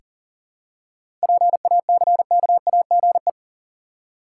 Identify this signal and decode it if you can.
Morse code, heard as a keyed tone, about 37 wpm, 700 Hz. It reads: PACKAGE
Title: Seven Letter Words - 40wpm